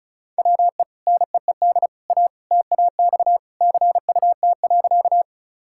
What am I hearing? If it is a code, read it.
Morse code, heard as a keyed tone, about 36 wpm, 700 Hz. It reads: WE NEED A TAX CUT.
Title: Sentences from Top 1000 Words - Review Part 5 - 35wpm